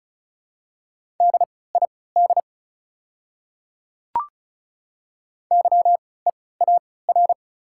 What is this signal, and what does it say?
Morse code, heard as a keyed tone, about 36 wpm, spaced as for 20 wpm, 690 Hz. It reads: DID YEAR
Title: 3x Character Spacing - Top 500 Words - 35wpm